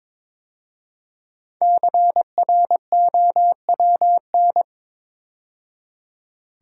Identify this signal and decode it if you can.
Morse code, heard as a keyed tone, about 22 wpm, 700 Hz. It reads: CROWN
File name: 5 Letter Words - First 3 Letters Given - 22wpm